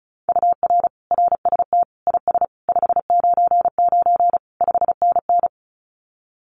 Morse code, heard as a keyed tone, about 34 wpm, 700 Hz. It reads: UR RST IS 599 5NN
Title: Common QSO Phrases Encore 35wpm